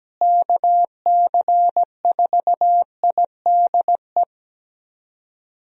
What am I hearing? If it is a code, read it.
Morse code, heard as a keyed tone, about 17 wpm, 700 Hz. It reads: KC4IDE
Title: USA Call Signs - 6 Characters - Repeated 2x - 17wpm